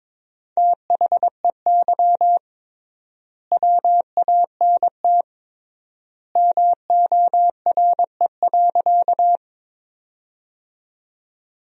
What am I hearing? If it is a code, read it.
Morse code, heard as a keyed tone, about 22 wpm, 700 Hz. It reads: THEY WANT MORE.
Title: 3x Word-Spacing - Sentences from Top 200 Words - 22wpm